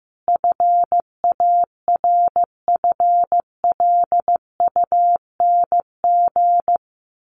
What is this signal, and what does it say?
Morse code, heard as a keyed tone, about 15 wpm, 700 Hz. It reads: FARFLUNG
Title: A Sound Of Thunder 15wpm